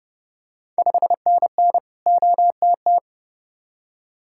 Morse code, heard as a keyed tone, about 30 wpm, 690 Hz. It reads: 5NN OTT
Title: Ontario QSO Party Exchange 30wpm